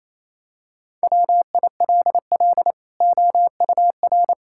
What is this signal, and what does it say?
Morse code, heard as a keyed tone, about 28 wpm, 700 Hz. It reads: WILL OUR
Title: Sentences from Top 1000 Words - Review Part 5 - 28wpm